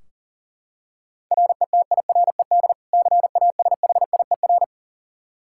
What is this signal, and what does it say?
Morse code, heard as a keyed tone, about 40 wpm, 700 Hz. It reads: RETIRED CASHIER
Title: K1USN SST QSO Practice 40wpm